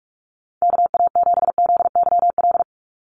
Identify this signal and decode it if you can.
Morse code, heard as a keyed tone, about 46 wpm, 690 Hz. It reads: KA7ZYL
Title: USA Call Signs 45wpm